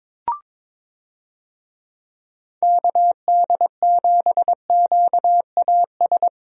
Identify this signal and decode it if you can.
Morse code, heard as a keyed tone, about 22 wpm, 700 Hz. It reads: KD7QAS
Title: USA Call Signs - 6 Characters - 22wpm